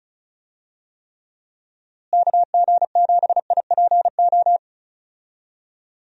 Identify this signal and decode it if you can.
Morse code, heard as a keyed tone, about 36 wpm, 700 Hz. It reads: KG7IPO
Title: USA Call Signs - 6 Characters - Repeated 2x - 35wpm